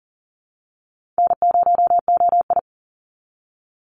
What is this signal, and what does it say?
Morse code, heard as a keyed tone, about 42 wpm, 700 Hz. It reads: N0OI
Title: NA SOTA Chaser Callsigns 40wpm